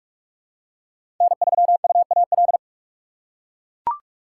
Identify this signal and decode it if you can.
Morse code, heard as a keyed tone, about 41 wpm, 700 Hz. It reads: N3UAL
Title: USA Call Signs - 5 Characters - 4x Character Spacing - 45wpm